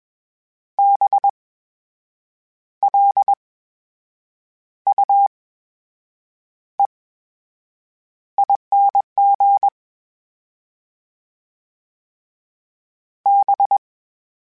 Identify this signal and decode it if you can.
Morse code, heard as a keyed tone, about 21 wpm, spaced as for 7 wpm, 800 Hz. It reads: BLUEING B
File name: Special INGs-mp3s / Special INGs-mp3s / VerySlowToFaster Ing @20